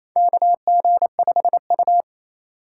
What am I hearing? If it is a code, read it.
Morse code, heard as a keyed tone, about 27 wpm, 700 Hz. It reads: KG5U
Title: USA Call Signs - 3x Character Spacing - 28wpm